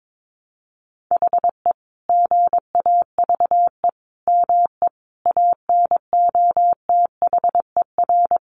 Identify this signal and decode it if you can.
Morse code, heard as a keyed tone, about 22 wpm, 700 Hz. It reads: HE GAVE ME ANOTHER
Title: Sentences from Top 1300 Words 22wpm